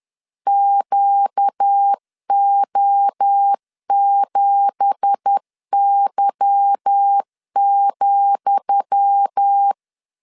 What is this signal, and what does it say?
Morse code, heard as a keyed tone, about 10 wpm, 800 Hz. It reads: QO7Y,
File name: LEZIONI / 10 SIGNS / 10D COMA 10wpm 3013